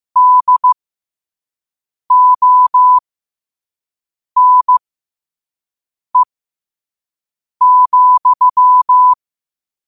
Morse code, heard as a keyed tone, about 15 wpm, spaced as for 5 wpm, 1000 Hz. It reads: DONE,